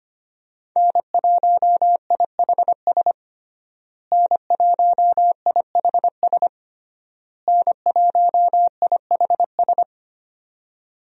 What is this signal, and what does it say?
Morse code, heard as a keyed tone, about 25 wpm, 700 Hz. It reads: N1IHS N1IHS N1IHS
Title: USA Call Signs - 5 Characters - Repeated 3x - 25wpm